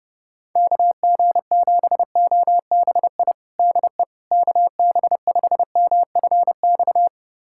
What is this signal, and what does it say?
Morse code, heard as a keyed tone, about 32 wpm, 700 Hz. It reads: KG7OBI DE KB5MFX
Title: USA Callsign DE Callsign - 6 Characters - 30wpm